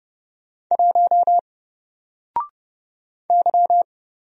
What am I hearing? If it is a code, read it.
Morse code, heard as a keyed tone, about 30 wpm, 700 Hz. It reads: 1 Y
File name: Single-Letter-Number-30wpm